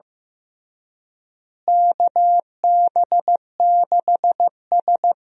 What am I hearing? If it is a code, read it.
Morse code, heard as a keyed tone, about 15 wpm, 700 Hz. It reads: KB6S
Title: USA Call Signs - 4 Characters - Repeated 3x - 15wpm